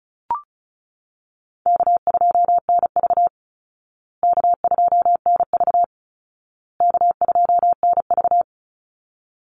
Morse code, heard as a keyed tone, about 34 wpm, 700 Hz. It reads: K2NV K2NV K2NV
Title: Easy Call Signs Repeated 3x - 35wpm